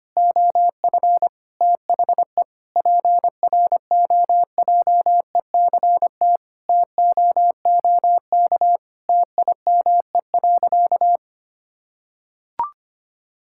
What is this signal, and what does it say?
Morse code, heard as a keyed tone, about 25 wpm, 700 Hz. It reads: OF THE PROJECT TOOK TIME.
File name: practice / 25wpm / Sentences with 3-Word Phrases - 25wpm